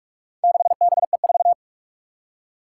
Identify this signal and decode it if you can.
Morse code, heard as a keyed tone, about 46 wpm, 700 Hz. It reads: 6BE4
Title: Four Challenging Letter-Number Combinations 45wpm